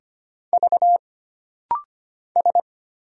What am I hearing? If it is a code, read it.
Morse code, heard as a keyed tone, about 25 wpm, 700 Hz. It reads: V S
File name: Single Character (Letter:Number:Punctuation:Prosign) - Rapid-Fire - 25wpm